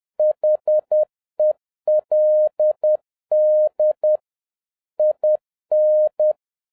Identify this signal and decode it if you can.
Morse code, heard as a keyed tone, about 10 wpm, 610 Hz. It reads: HELD IN
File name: 2011-07-mp3 / n46 JOHN K3TN 10wpm